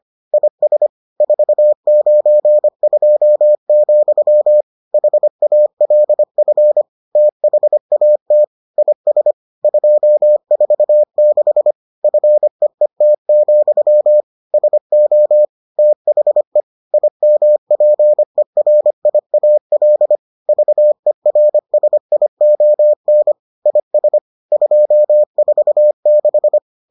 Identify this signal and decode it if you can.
Morse code, heard as a keyed tone, about 25 wpm, 600 Hz. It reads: IS 492, HALF THAT IS 246 FEET, SO THE IMPERIAL VERSION IS 246